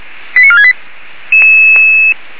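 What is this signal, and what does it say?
Five-tone selective call (ZVEI), heard as quick runs of five tones, tones of about 70 ms. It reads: 89468
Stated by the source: ZVEI five-tone selective call received on SDR.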